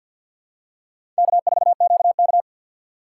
Morse code, heard as a keyed tone, about 47 wpm, 700 Hz. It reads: K4QK
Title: USA Call Signs - 4 Characters - 2x Character Spacing - 50wpm